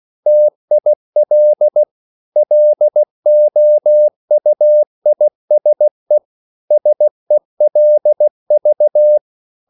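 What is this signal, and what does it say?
Morse code, heard as a keyed tone, about 16 wpm, 600 Hz. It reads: TIL LOUISE SELV
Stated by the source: Slow (QRS) Morse code at 16 wpm, from CW med Gnister.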